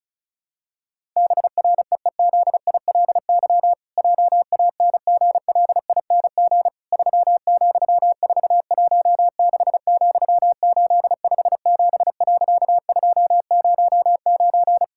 Morse code, heard as a keyed tone, about 34 wpm, 700 Hz. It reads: BREEZILY JANGLING 3,416,857.209
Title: Pangram Challenge v2 - 35wpm